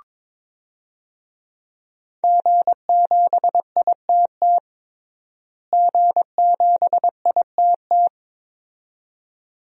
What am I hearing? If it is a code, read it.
Morse code, heard as a keyed tone, about 22 wpm, 700 Hz. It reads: G7ITT G7ITT
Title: DXCC-100 Call Signs - 5 Characters - Repeated 2x - 22wpm